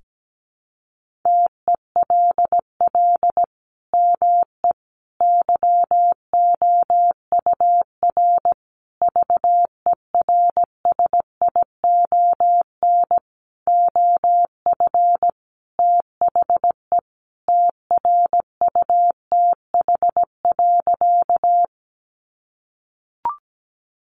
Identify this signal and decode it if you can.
Morse code, heard as a keyed tone, about 17 wpm, 700 Hz. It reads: TELL ME YOUR VERSION OF THE TRUTH.
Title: Sentences from Top 1400 Words 17wpm